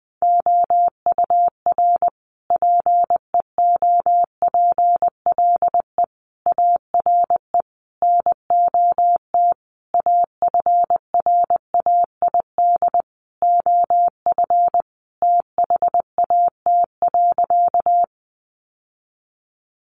Morse code, heard as a keyed tone, about 20 wpm, 700 Hz. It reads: OUR PEOPLE ARE NOT AFRAID OF THAT.
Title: Sentences from Top 1500 Words 20wpm